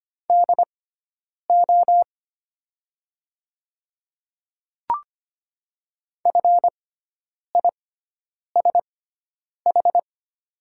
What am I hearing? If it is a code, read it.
Morse code, heard as a keyed tone, about 25 wpm, spaced as for 9 wpm, 690 Hz. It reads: DO FISH
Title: 6x Character Spacing - Top 500 Words - 25wpm